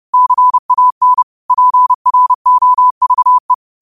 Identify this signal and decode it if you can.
Morse code, heard as a keyed tone, about 30 wpm, 1000 Hz. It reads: CAN PROVE
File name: 30wpm